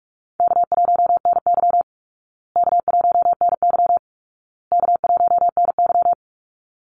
Morse code, heard as a keyed tone, about 44 wpm, 700 Hz. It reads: K1NY K1NY K1NY
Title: Easy Call Signs Repeated 3x - 45wpm